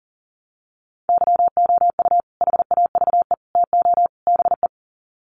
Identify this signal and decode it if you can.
Morse code, heard as a keyed tone, about 40 wpm, 700 Hz. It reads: YOU HAVE TO BE
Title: Sets of 4 Words 40wpm